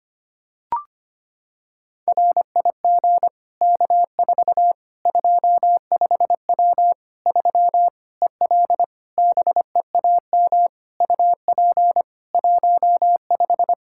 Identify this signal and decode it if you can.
Morse code, heard as a keyed tone, about 25 wpm, 700 Hz. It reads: RIG K4 25W 3 EL BEAM UP 15
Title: Practice QSO - Exchange 2 - Info - 25wpm